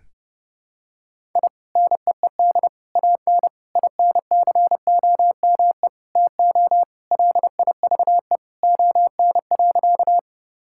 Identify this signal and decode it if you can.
Morse code, heard as a keyed tone, about 30 wpm, 700 Hz. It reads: I NEED AN INCOME TO LIVE ON.
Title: Sentences from Top 1000 Words - Review Part 5 - 30wpm